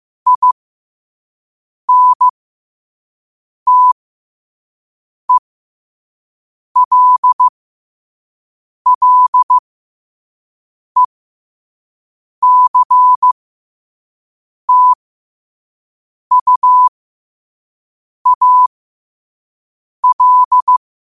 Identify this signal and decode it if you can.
Morse code, heard as a keyed tone, about 15 wpm, spaced as for 5 wpm, 1000 Hz. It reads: INTELLECTUAL
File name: qotdfiles / 5wpm